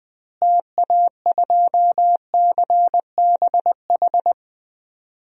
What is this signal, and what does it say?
Morse code, heard as a keyed tone, about 20 wpm, 700 Hz. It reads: TA2CBH
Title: DXCC-100 Call Signs - 6 Characters - Repeated 2x - 20wpm